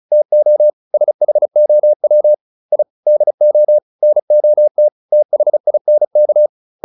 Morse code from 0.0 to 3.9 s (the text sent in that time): TO SHOW I DO